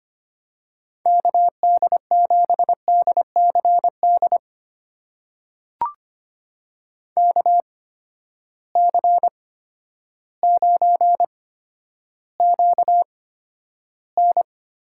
Morse code, heard as a keyed tone, about 25 wpm, spaced as for 11 wpm, 700 Hz. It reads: KD7DCD KC9QN
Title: USA Call Signs - 6 Characters - 8x Character Spacing - 25wpm